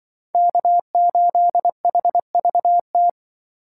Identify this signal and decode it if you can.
Morse code, heard as a keyed tone, about 23 wpm, 700 Hz. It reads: K8HVT
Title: Speed Racing - USA 1x3 Call Signs 20wpm